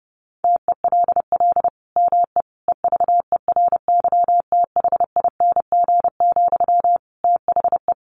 Morse code, heard as a keyed tone, about 30 wpm, 700 Hz. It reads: TELL ME EVERYTHING, THE
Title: Sentences from Top 800 Words 30wpm